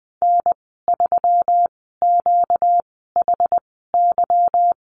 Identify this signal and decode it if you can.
Morse code, heard as a keyed tone, about 20 wpm, spaced as for 15 wpm, 700 Hz. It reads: N3QHY
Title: 238 Lesson 38g - Letter q - Call Signs 20wpm